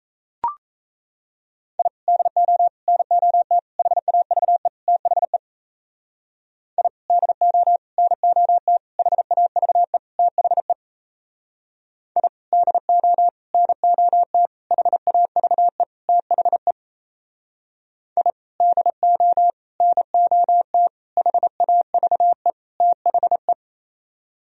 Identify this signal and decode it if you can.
Morse code, heard as a keyed tone, about 37 wpm, 700 Hz. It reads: I DO NOT HAVE THE I DO NOT HAVE THE I DO NOT HAVE THE I DO NOT HAVE THE